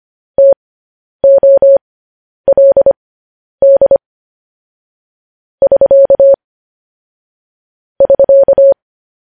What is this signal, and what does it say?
Morse code, heard as a keyed tone, about 25 wpm, spaced as for 10 wpm, 550 Hz. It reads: TOLD <SK> <SK>
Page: news headlines in morse code at 10 wpm